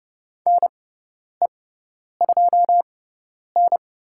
Morse code, heard as a keyed tone, about 30 wpm, spaced as for 10 wpm, 700 Hz. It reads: NE2N
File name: Lesson 17g - Letter u - Call Signs 30wpm